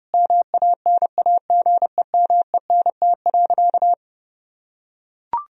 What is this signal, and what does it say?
Morse code, heard as a keyed tone, about 30 wpm, 700 Hz. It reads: MANAGEMENT.
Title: Sentences from Top 1000 Words - Review Part 4 - 30wpm